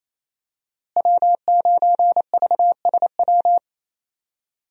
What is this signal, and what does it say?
Morse code, heard as a keyed tone, about 27 wpm, 700 Hz. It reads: W9VSW
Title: USA Call Signs - 5 Characters - Repeated 3x - 28wpm